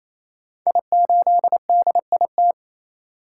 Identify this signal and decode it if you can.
Morse code, heard as a keyed tone, about 27 wpm, 700 Hz. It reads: I8DIT
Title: Code Groups - 5 Letter:Numbers - 28wpm